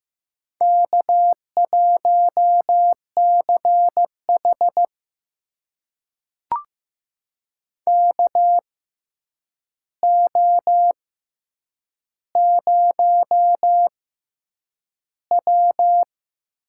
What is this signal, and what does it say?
Morse code, heard as a keyed tone, about 15 wpm, spaced as for 8 wpm, 700 Hz. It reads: K1CH KO0W
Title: USA Call Signs - 4 Characters - 6x Character Spacing - 15wpm